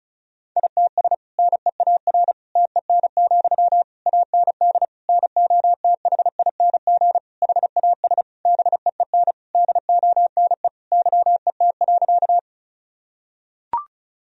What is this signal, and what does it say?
Morse code, heard as a keyed tone, about 34 wpm, 700 Hz. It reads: ITS NEAR TEN, AND NOTHING HAS BEEN DONE YET.
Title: Sentences from Top 1000 Words - Review Part 2 - 35wpm